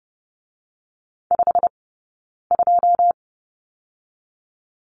Morse code, heard as a keyed tone, about 30 wpm, spaced as for 9 wpm, 700 Hz. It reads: 52
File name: Lesson 20d - Number 3 - Two Characters 30wpm